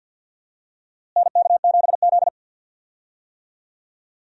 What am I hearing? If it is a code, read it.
Morse code, heard as a keyed tone, about 52 wpm, 700 Hz. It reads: NK7Z
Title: USA Call Signs - 4 Characters - Repeated 3x - 50wpm